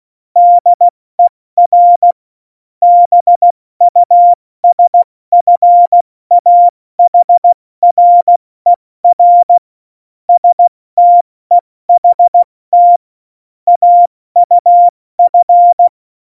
Morse code, heard as a keyed tone, about 16 wpm, spaced as for 14 wpm, 700 Hz. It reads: DER BUSFAHRER STEHT AUF
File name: Morsekurs / Witze / Witz 29. Januar 2019, 70 F80